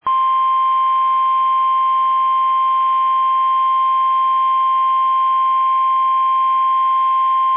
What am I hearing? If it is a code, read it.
Emergency Alert System (SAME) tones (the 1050 Hz attention tone). It is heard from 0.1 to 6.5 s.
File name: eas / KUSO / KUSO